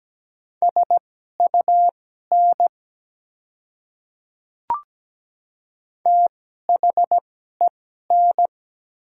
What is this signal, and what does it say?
Morse code, heard as a keyed tone, about 17 wpm, spaced as for 12 wpm, 690 Hz. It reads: SUN THEN
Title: 2x Character Spacing - Top 300 Words - 17wpm